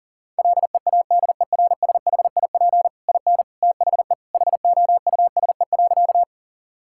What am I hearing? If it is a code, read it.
Morse code, heard as a keyed tone, about 37 wpm, 700 Hz. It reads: LEADERSHIP IN THE HOUSE.
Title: Sentences from Top 1600 Words 40wpm